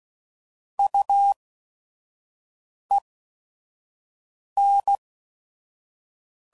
Morse code, heard as a keyed tone, about 16 wpm, spaced as for 5 wpm, 800 Hz. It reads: UEN
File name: CW Code Groups @5WPM / Technical Subjects / XC